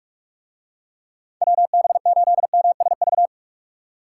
Morse code, heard as a keyed tone, about 44 wpm, 700 Hz. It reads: WB8MSV